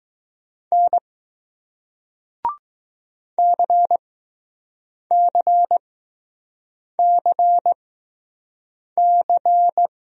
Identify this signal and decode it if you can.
Morse code, heard as a keyed tone, about 20 wpm, spaced as for 6 wpm, 700 Hz. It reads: N CCCC